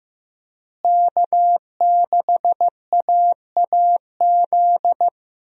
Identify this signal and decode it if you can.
Morse code, heard as a keyed tone, about 16 wpm, 700 Hz. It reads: K6AAZ
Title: USA Call Signs - 5 Characters - Repeated 3x - 15wpm